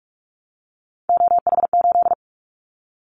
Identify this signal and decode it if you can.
Morse code, heard as a keyed tone, about 44 wpm, 700 Hz. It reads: OH8
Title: Callsign Prefixes for Top 25 DXCC - 45wpm